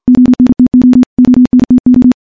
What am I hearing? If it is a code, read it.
Morse code, heard as a keyed tone, about 50 wpm, 250 Hz, with radio noise. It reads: ONTO ONTO
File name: morse / morse-training-50WPM